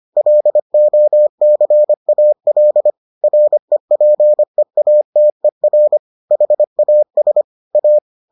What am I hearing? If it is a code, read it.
Morse code, heard as a keyed tone, about 25 wpm, 600 Hz. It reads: LOCAL REPEATER HAS A